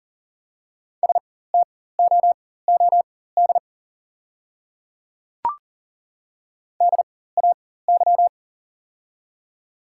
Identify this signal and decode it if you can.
Morse code, heard as a keyed tone, about 40 wpm, spaced as for 19 wpm, 690 Hz. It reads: STOOD DAY
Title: 4x Character Spacing - Top 500 Words - 40wpm